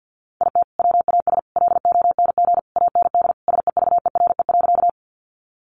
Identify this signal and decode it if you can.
Morse code, heard as a keyed tone, about 50 wpm, 700 Hz. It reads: IT WAS LONG AND SEVERE.
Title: Sentences from Top 1400 Words 50wpm